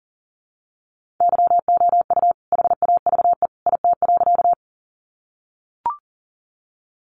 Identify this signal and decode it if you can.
Morse code, heard as a keyed tone, about 40 wpm, 700 Hz. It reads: YOU HAVE IT.
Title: Sentences from Top 100 Words 40wpm